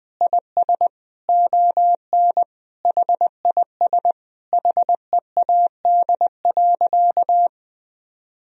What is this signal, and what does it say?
Morse code, heard as a keyed tone, about 20 wpm, 700 Hz. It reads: IS ON HIS HEAD.
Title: Sentences from Top 1000 Words - Review Part 4 - 20wpm